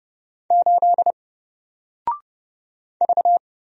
Morse code, heard as a keyed tone, about 31 wpm, 700 Hz. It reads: 8 V
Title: Single Letter-Number 30wpm